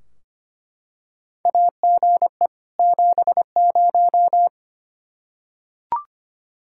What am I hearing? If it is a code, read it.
Morse code, heard as a keyed tone, about 25 wpm, 700 Hz. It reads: AGE 70
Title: Practice QSO - Exchange 3 - More Info - 25wpm